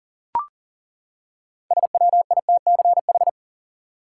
Morse code, heard as a keyed tone, about 37 wpm, 700 Hz. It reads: SWITCH
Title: Six Letter Words - 40wpm